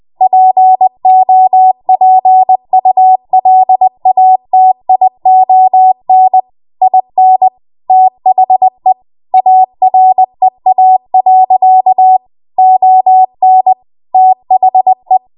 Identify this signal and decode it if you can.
Morse code, heard as a keyed tone, about 20 wpm, 750 Hz. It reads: POPULATION IN THE AREA. ON THE
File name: Morse / 20WPM